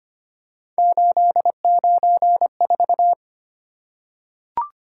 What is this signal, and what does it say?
Morse code, heard as a keyed tone, about 25 wpm, 700 Hz. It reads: 894
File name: Three Letter-Number Combinations - 25wpm